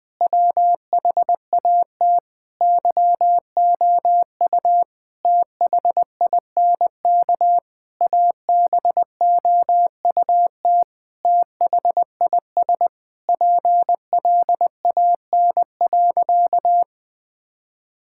Morse code, heard as a keyed tone, about 20 wpm, 700 Hz. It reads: WHAT YOU THINK ABOUT THIS PLAN.
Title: Sentences with 9-Word Phrases - 20wpm